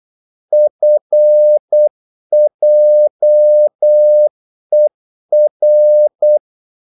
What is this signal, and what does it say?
Morse code, heard as a keyed tone, about 8 wpm, 600 Hz. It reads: FJER
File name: Korte ord DK 8wpm